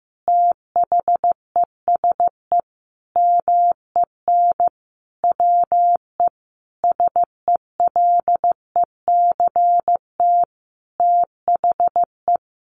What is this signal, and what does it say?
Morse code, heard as a keyed tone, about 15 wpm, 700 Hz. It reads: THESE MEN WE SELECT THE
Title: Sentences from Top 1300 Words 15wpm